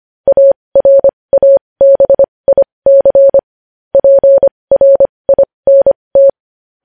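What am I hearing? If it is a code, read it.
Morse code, heard as a keyed tone, about 25 wpm, spaced as for 20 wpm, 550 Hz. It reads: ARABIC PRINT
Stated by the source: Morse code news headlines sent at 20.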